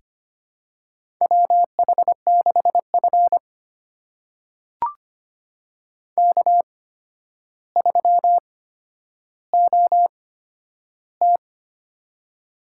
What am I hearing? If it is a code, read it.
Morse code, heard as a keyed tone, about 25 wpm, spaced as for 11 wpm, 700 Hz. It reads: WH6F K3OT
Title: USA Call Signs - 8x Character Spacing - 25wpm